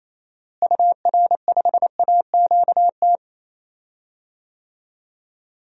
Morse code, heard as a keyed tone, about 27 wpm, 700 Hz. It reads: UR5AQT
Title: DXCC-100 Call Signs - 6 Characters - Repeated 2x - 28wpm